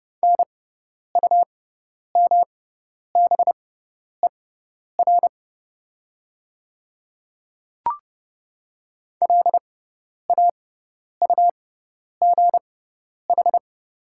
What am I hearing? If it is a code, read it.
Morse code, heard as a keyed tone, about 30 wpm, spaced as for 10 wpm, 700 Hz. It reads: NUMBER LAUGH
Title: 6x Character Spacing - Top 500 Words - 30wpm